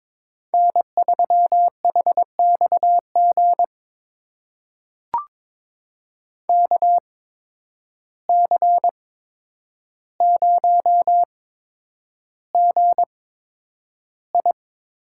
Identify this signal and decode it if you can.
Morse code, heard as a keyed tone, about 22 wpm, spaced as for 9 wpm, 700 Hz. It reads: N3HXG KC0GI
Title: USA Call Signs - 8x Character Spacing - 22wpm